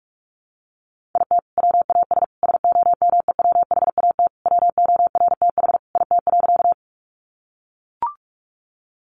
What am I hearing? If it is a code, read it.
Morse code, heard as a keyed tone, about 44 wpm, 700 Hz. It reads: IT WAS SOMEWHAT WORTH IT.
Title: Sentences from Top 1300 Words 45wpm